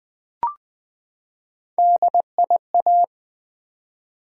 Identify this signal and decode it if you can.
Morse code, heard as a keyed tone, about 20 wpm, 700 Hz. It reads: DIA
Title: Build A Word - 3 Letters At A Time - 9-Letter Words - 20wpm